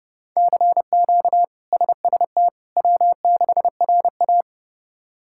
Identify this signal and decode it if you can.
Morse code, heard as a keyed tone, about 31 wpm, 700 Hz. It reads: CQ SST W6RA
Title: K1USN SST QSO Practice - 30wpm